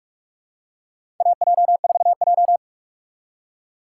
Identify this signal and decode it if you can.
Morse code, heard as a keyed tone, about 48 wpm, 700 Hz. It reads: AJ4J
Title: USA Call Signs - 4 Characters - 2x Character Spacing - 45wpm